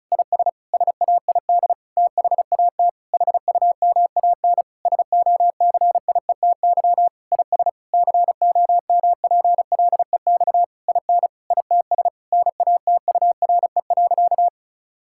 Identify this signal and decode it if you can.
Morse code, heard as a keyed tone, about 36 wpm, 700 Hz. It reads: IS SAID THAT HUMAN SOCIETY IS COMPLEX IN ITS NATURE.
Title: Sentences from Top 1000 Words - Review Part 6 - 35wpm